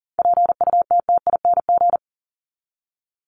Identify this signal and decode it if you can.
Morse code, heard as a keyed tone, about 40 wpm, 700 Hz. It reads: PUTTING
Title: ING Suffix - Top 100 Words with Base Words 40wpm